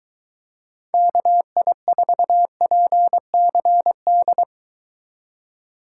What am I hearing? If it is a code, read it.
Morse code, heard as a keyed tone, about 23 wpm, 700 Hz. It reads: KI4PCD